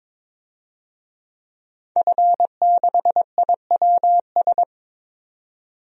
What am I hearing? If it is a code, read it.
Morse code, heard as a keyed tone, about 22 wpm, 700 Hz. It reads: F6IWS
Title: DXCC-100 Call Signs - 5 Characters - 22wpm